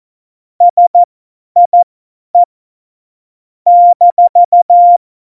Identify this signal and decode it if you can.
Morse code, heard as a keyed tone, about 14 wpm, spaced as for 10 wpm, 700 Hz. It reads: SIE -
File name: Morsekurs / Witze / Witz 11 Juli 17, Tempo 50 F70